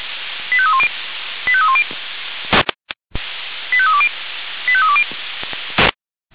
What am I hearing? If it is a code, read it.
Five-tone selective call (ZVEI), heard as quick runs of five tones, tones of about 70 ms. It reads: 85310 85310 85320 85320